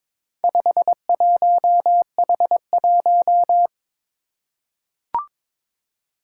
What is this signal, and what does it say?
Morse code, heard as a keyed tone, about 22 wpm, 700 Hz. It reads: 51H1
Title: Four Challenging Letter-Number Combinations 22wpm